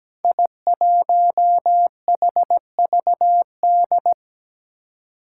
Morse code, heard as a keyed tone, about 17 wpm, 700 Hz. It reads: I1HVD
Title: Five Challenging Letter-Number Combinations 17wpm